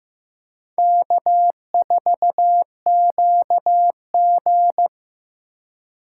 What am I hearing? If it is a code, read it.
Morse code, heard as a keyed tone, about 19 wpm, 700 Hz. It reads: K4QG